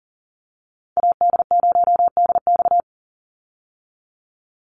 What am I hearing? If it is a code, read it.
Morse code, heard as a keyed tone, about 40 wpm, 700 Hz. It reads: AD0DX
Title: Call Signs Repeated 2x - 40wpm